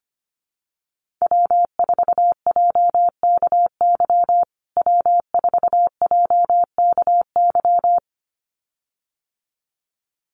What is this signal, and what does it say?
Morse code, heard as a keyed tone, about 25 wpm, 700 Hz. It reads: W4JKY W4JKY
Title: USA Call Signs Repeated 2x 25wpm